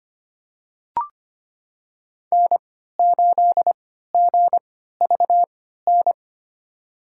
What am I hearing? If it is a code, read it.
Morse code, heard as a keyed tone, about 25 wpm, spaced as for 14 wpm, 700 Hz. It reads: N8GVN
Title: USA Call Signs - 5 Characters - 3x Character Spacing - 25wpm